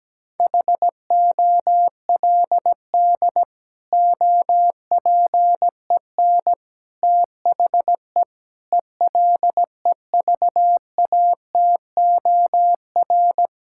Morse code, heard as a keyed tone, about 17 wpm, 700 Hz. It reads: HOLD OPEN THE ELEVATOR
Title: Sentences from Top 1600 Words 17wpm